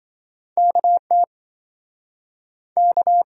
Morse code, heard as a keyed tone, about 27 wpm, 700 Hz. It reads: KT K